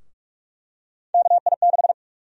Morse code, heard as a keyed tone, about 43 wpm, 700 Hz. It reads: KI6
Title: USA Call Sign Prefixes - Repeated 2x - 45wpm